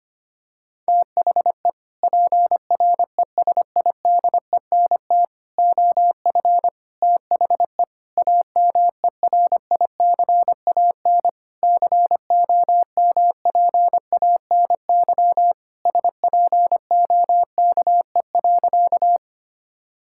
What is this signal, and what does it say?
Morse code, heard as a keyed tone, about 25 wpm, 700 Hz. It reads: THE PRESIDENT OF THE AMERICAN COMPANY SPOKE.
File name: practice / 25wpm / Sentences with 3-Word Phrases - 25wpm